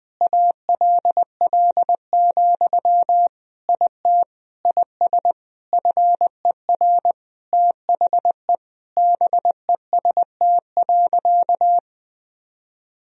Morse code, heard as a keyed tone, about 20 wpm, 700 Hz. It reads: ALL, IT IS FER THE BEST.
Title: Sentences from Top 500 Words and Abbreviations 20wpm